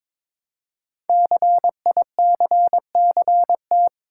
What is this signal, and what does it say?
Morse code, heard as a keyed tone, about 22 wpm, 700 Hz. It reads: CICCT
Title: Code Groups - 5 Letters - 22wpm